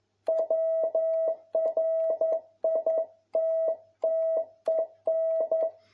Morse code, heard as a keyed tone, about 11 wpm, 650 Hz. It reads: WRITTEN
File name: jh8cbh / 2021cw